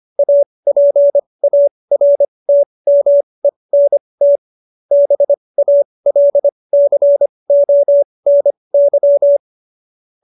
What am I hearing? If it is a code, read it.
Morse code, heard as a keyed tone, about 25 wpm, spaced as for 20 wpm, 570 Hz. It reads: APARTMENT BALCONY